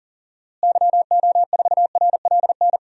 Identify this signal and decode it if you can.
Morse code, heard as a keyed tone, about 40 wpm, 700 Hz. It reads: YO4RLN
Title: DXCC-100 Call Signs - 6 Characters - Repeated 2x - 40wpm